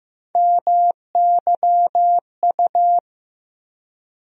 Morse code, heard as a keyed tone, about 15 wpm, 700 Hz. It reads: MYU
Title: USA Call Sign Postfixes - 3 Letters 15wpm